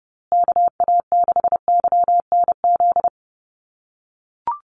Morse code, heard as a keyed tone, about 30 wpm, 700 Hz. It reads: KA6YNZ
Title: USA Call Signs Repeated 2x 30wpm